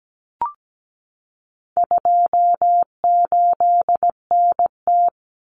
Morse code, heard as a keyed tone, about 17 wpm, 700 Hz. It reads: 28NT
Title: Common Cut Numbers - 4-Digit 17wpm